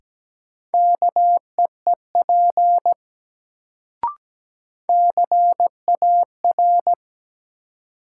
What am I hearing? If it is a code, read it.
Morse code, heard as a keyed tone, about 17 wpm, 700 Hz. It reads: KEEP CAR
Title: Sentences from Top 300 Words - New Word Drill - 17wpm